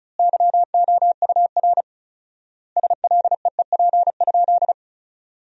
Morse code, heard as a keyed tone, about 34 wpm, 700 Hz. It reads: YOUR SLEEP?
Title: 4x Word-Spacing - Sentences from Top 500 Words - 35wpm